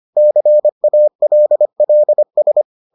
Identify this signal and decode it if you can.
Morse code, heard as a keyed tone, about 25 wpm, 600 Hz. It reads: CALLS